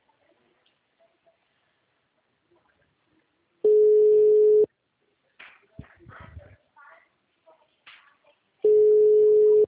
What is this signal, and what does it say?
Ringback tone (Europe, 425 Hz), with speech.
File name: out- -2003- - - .57123